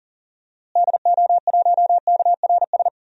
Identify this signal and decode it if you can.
Morse code, heard as a keyed tone, about 39 wpm, 700 Hz. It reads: DO1KRS
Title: DXCC-100 Call Signs - 6 Characters - Repeated 3x - 40wpm